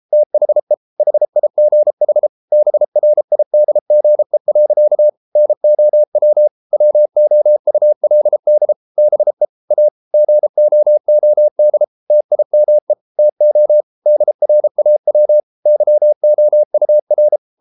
Morse code, heard as a keyed tone, about 33 wpm, 600 Hz. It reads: THE HIGH BRIDGE. NOW WOULD BE A GOOD TIME TO DRAW YOUR